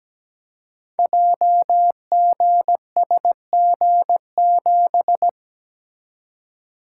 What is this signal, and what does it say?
Morse code, heard as a keyed tone, about 17 wpm, 700 Hz. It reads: JGSG7
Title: Code Groups - 5 Letter:Numbers - 17wpm